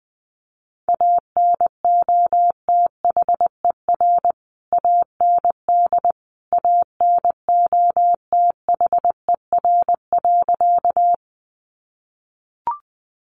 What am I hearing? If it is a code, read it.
Morse code, heard as a keyed tone, about 20 wpm, 700 Hz. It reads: ANOTHER AND ANOTHER.
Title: Sentences from Top 600 Words 20wpm